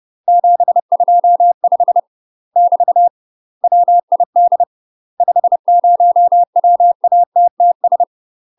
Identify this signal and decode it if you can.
Morse code, heard as a keyed tone, about 30 wpm, 700 Hz. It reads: 725 <BT> WID 50 WATTS
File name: UK MASTER QSOs TRAINING / 30WPM / UK MASTER QSOs 0031